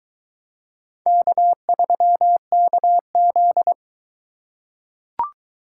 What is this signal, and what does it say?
Morse code, heard as a keyed tone, about 22 wpm, 700 Hz. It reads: K3KZ